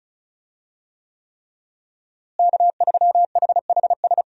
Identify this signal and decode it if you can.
Morse code, heard as a keyed tone, about 34 wpm, 700 Hz. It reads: K3HHS
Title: USA Call Signs - 5 Characters - Repeated 3x - 35wpm